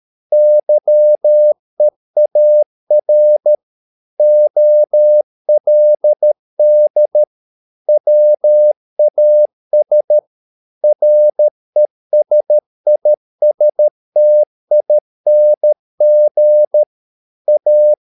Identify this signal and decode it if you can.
Morse code, heard as a keyed tone, about 13 wpm, 600 Hz. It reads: YEAR OLD WAS RESISTING A
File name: mp3 files / Lexiphiles - 13wpm